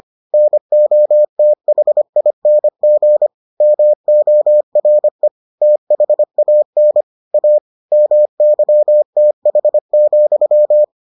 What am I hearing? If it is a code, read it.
Morse code, heard as a keyed tone, about 25 wpm, 600 Hz. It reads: NOTHING MORE THAN A MYTH,